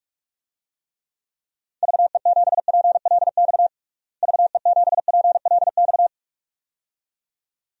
Morse code, heard as a keyed tone, about 42 wpm, 700 Hz. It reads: VE7PLX VE7PLX
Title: DXCC-100 Call Signs - 6 Characters - Repeated 2x - 45wpm